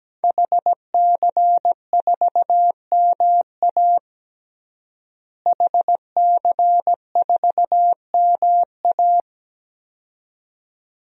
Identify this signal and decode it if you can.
Morse code, heard as a keyed tone, about 17 wpm, 700 Hz. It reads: HC4MA HC4MA
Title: DXCC-100 Call Signs - 5 Characters - Repeated 2x - 17wpm